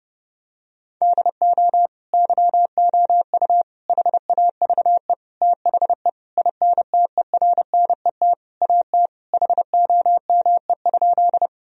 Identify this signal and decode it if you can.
Morse code, heard as a keyed tone, about 32 wpm, 700 Hz. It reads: DO YOU HAVE THE INTERNET AT HOME?
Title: Sentences from Top 1000 Words - Review Part 4 - 30wpm